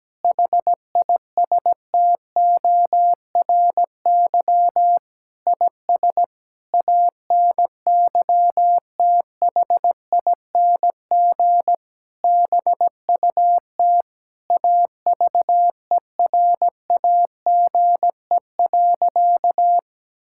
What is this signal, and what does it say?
Morse code, heard as a keyed tone, about 17 wpm, 700 Hz. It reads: HISTORY IS ANYTHING BUT AVERAGE.
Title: Sentences from Top 1000 Words - Review Part 6 - 17wpm